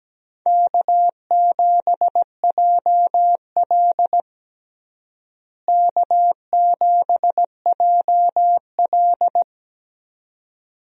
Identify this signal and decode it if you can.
Morse code, heard as a keyed tone, about 17 wpm, 700 Hz. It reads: K7JL K7JL
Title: USA Call Signs - 4 Characters - Repeated 2x - 17wpm